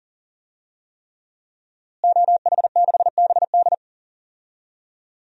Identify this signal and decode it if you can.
Morse code, heard as a keyed tone, about 37 wpm, 700 Hz. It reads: OH6BD
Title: DXCC-100 Call Signs - 5 Characters - Repeated 3x - 40wpm